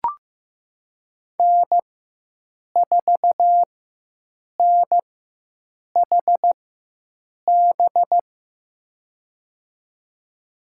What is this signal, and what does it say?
Morse code, heard as a keyed tone, about 15 wpm, spaced as for 7 wpm, 700 Hz. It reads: N4NHB
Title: USA Call Signs - 5 Characters - 4x Character Spacing - 15wpm